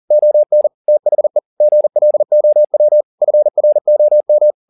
Morse code, heard as a keyed tone, about 40 wpm, 600 Hz. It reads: ON THE GLOW FROM